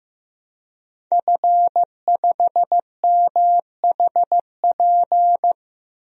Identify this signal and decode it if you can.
Morse code, heard as a keyed tone, about 16 wpm, 700 Hz. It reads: F5MHP
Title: DXCC-100 Call Signs - 5 Characters - Repeated 2x - 15wpm